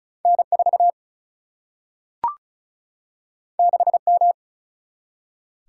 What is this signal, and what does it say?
Morse code, heard as a keyed tone, about 35 wpm, 700 Hz. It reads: N4 6M
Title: Single-Letter Single-Number 35wpm